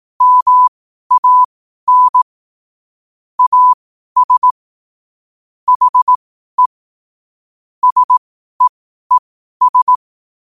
Morse code, heard as a keyed tone, about 18 wpm, spaced as for 12 wpm, 1000 Hz. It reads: MAN AS HE SEES
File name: qotdfiles / 12wpm